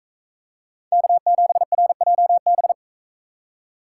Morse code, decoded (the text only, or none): K7RJB